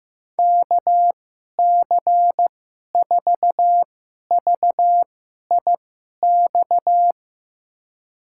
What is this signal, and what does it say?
Morse code, heard as a keyed tone, about 15 wpm, spaced as for 11 wpm, 700 Hz. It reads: KC4VIX
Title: USA Call Signs - 6 Characters - 2x Character Spacing - 15wpm